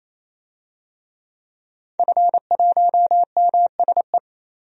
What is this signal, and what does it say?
Morse code, heard as a keyed tone, about 28 wpm, 700 Hz. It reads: F1MSE
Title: DXCC-100 Call Signs - 5 Characters - 28wpm